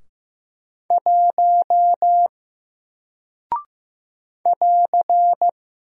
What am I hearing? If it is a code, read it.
Morse code, heard as a keyed tone, about 15 wpm, 700 Hz. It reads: 1 <AR>
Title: Single Character (Letter:Number:Punctuation:Prosign) Alt 15wpm